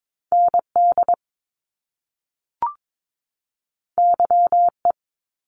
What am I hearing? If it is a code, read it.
Morse code, heard as a keyed tone, about 22 wpm, 700 Hz. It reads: ND YE
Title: Top Two Letter Combinations in Words 22wpm